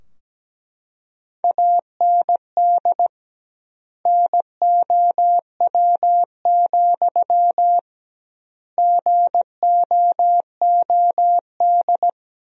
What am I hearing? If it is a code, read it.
Morse code, heard as a keyed tone, about 17 wpm, 700 Hz. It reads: AND NOW, GOOD
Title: 2x Word-Spacing - Sentences from Top 300 Words - 17wpm